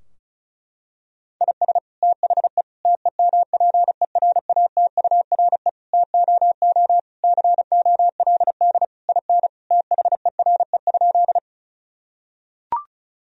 Morse code, heard as a keyed tone, about 34 wpm, 700 Hz. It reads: IS THE TEMPERATURE TOO COLD IN THERE?
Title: Sentences from Top 1000 Words - Review Part 4 - 35wpm